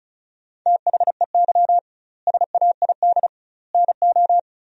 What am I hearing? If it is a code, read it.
Morse code, heard as a keyed tone, about 36 wpm, 700 Hz. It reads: THEY SAID NO
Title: 2x Word-Spacing - Sentences from Top 100 Words - 35wpm